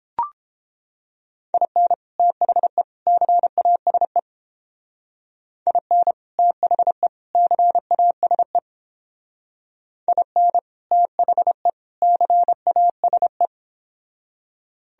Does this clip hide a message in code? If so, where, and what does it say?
Yes: Morse code from 1.5 to 13.5 s, heard as a keyed tone, about 29 wpm, 700 Hz. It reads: IN THE CASE IN THE CASE IN THE CASE